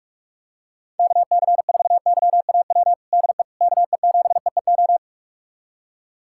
Morse code, heard as a keyed tone, about 42 wpm, 700 Hz. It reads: KC4YAW DE KE7EEO
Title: USA Callsign DE Callsign 45wpm